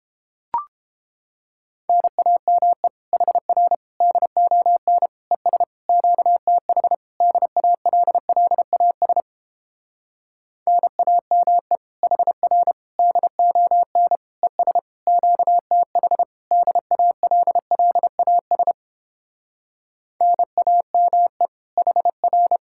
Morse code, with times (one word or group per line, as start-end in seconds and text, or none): 1.9-2.9 NAME
3.1-3.7 HR
4.0-5.1 DON
5.3-5.6 ES
5.9-6.9 QTH
7.2-9.2 DALLAS
10.7-11.8 NAME
12.0-12.7 HR
13.0-14.2 DON
14.4-14.8 ES
15.1-16.2 QTH
16.5-18.7 DALLAS
20.2-21.5 NAME
21.8-22.6 HR